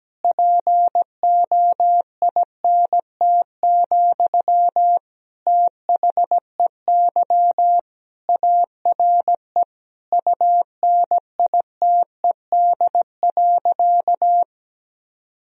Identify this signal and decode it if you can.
Morse code, heard as a keyed tone, about 17 wpm, 700 Hz. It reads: POINT, THEY ARE UNITED.
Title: Sentences from Top 1000 Words - Review Part 4 - 17wpm